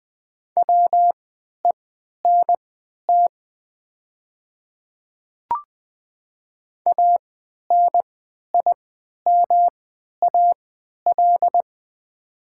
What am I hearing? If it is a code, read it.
Morse code, heard as a keyed tone, about 20 wpm, spaced as for 11 wpm, 700 Hz. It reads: WENT ANIMAL
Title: 3x Character Spacing - Top 200 Words - 20wpm